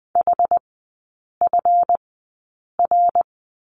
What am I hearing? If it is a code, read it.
Morse code, heard as a keyed tone, about 20 wpm, spaced as for 8 wpm, 700 Hz. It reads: HFR
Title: 172 Lesson 28e - Number 9 - Three Characters 20wpm